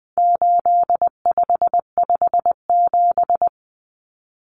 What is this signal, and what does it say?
Morse code, heard as a keyed tone, about 20 wpm, 700 Hz. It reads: 8557
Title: Numbers - 4-Digit 20wpm